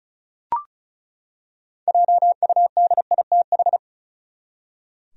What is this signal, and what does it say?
Morse code, heard as a keyed tone, about 34 wpm, 700 Hz. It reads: JUDITH
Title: USA Common Names - 35wpm